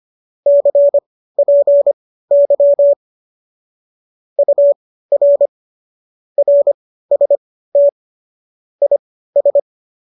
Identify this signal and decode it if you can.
Morse code, heard as a keyed tone, about 25 wpm, spaced as for 15 wpm, 570 Hz. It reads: CPY UR RST IS